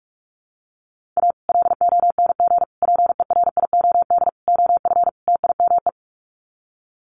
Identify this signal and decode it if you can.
Morse code, heard as a keyed tone, about 44 wpm, 700 Hz. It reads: A LONG PERIOD OF TIME
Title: Sets of 5 Words 45wpm